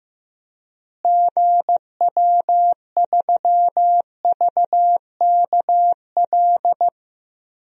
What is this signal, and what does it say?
Morse code, heard as a keyed tone, about 15 wpm, 700 Hz. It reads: GW3VKL
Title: DXCC-100 Call Signs - 6 Characters - Repeated 3x - 15wpm